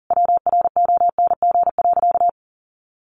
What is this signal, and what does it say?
Morse code, heard as a keyed tone, about 40 wpm, 700 Hz. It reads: WRONG.
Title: Easy Phrases 40wpm